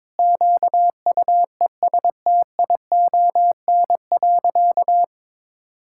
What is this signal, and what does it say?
Morse code, heard as a keyed tone, about 22 wpm, 700 Hz. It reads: QUESTION.
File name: practice / 22wpm / Sentences with 3-Word Phrases - 22wpm